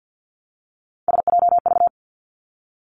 Morse code, heard as a keyed tone, about 50 wpm, 700 Hz. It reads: SJV
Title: ARRL Sections 50wpm